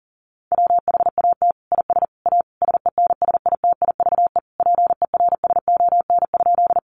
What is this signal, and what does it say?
Morse code, heard as a keyed tone, about 40 wpm, 700 Hz. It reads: WHAT IS A SENSITIVE PERSON?
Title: Sentences from Top 1400 Words 40wpm